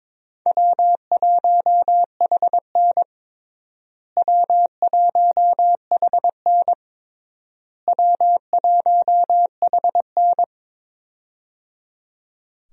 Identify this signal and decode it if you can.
Morse code, heard as a keyed tone, about 22 wpm, 700 Hz. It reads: W1HN W1HN W1HN
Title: USA Call Signs - 4 Characters - Repeated 3x - 22wpm